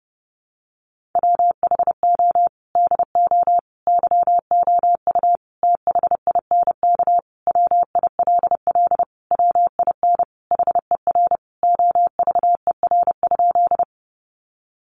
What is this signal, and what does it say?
Morse code, heard as a keyed tone, about 30 wpm, 700 Hz. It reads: WHO DO YOU THINK WILL WIN HER OVER?
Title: Sentences from Top 1500 Words 30wpm